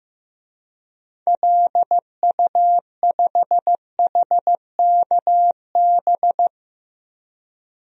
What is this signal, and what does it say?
Morse code, heard as a keyed tone, about 15 wpm, 700 Hz. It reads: LU5HKB
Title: DXCC-100 Call Signs - 6 Characters - Repeated 3x - 15wpm